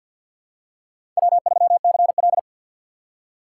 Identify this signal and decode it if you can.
Morse code, heard as a keyed tone, about 51 wpm, 700 Hz. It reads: W3CL
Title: USA Call Signs - 4 Characters - 2x Character Spacing - 50wpm